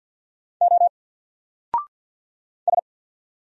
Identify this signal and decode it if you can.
Morse code, heard as a keyed tone, about 48 wpm, 700 Hz. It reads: O S
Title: Single Letter-Number 50wpm